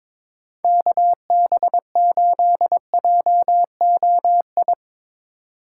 Morse code, heard as a keyed tone, about 22 wpm, 700 Hz. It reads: KB8JOI
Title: USA Call Signs - 6 Characters - 6x Character Spacing - 22wpm